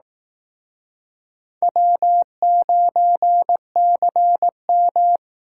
Morse code, heard as a keyed tone, about 18 wpm, 700 Hz. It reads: W9CM
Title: Speed Racing - USA 1x2 Call Signs 15wpm